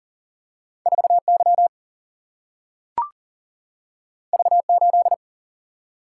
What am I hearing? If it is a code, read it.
Morse code, heard as a keyed tone, about 40 wpm, 700 Hz. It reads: 4 Y V 8
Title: Single-Letter Single-Number 40wpm